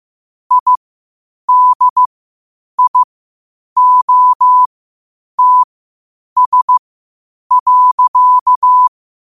Morse code, heard as a keyed tone, about 15 wpm, spaced as for 8 wpm, 1000 Hz. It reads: IDIOTS.
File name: qotdfiles / 8wpm